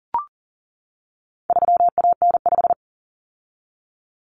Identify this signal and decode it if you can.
Morse code, heard as a keyed tone, about 40 wpm, 700 Hz. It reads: 3AN5
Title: Common Cut Numbers - 4-Digit 40wpm